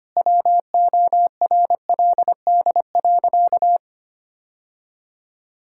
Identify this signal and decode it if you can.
Morse code, heard as a keyed tone, about 25 wpm, 700 Hz. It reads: WORLD.
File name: practice / 25wpm / Sentences with 4-Word Phrases - 25wpm